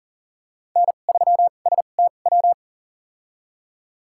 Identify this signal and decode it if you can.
Morse code, heard as a keyed tone, about 38 wpm, spaced as for 28 wpm, 700 Hz. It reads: N3STW
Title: USA Call Signs - 5 Characters - 2x Character Spacing - 40wpm